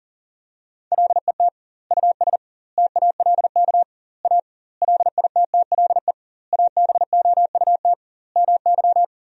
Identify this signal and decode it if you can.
Morse code, heard as a keyed tone, about 40 wpm, 700 Hz. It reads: LET US TALK A LITTLE ABOUT MY
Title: 2x Word-Spacing - Sentences from Top 400 Words - 40wpm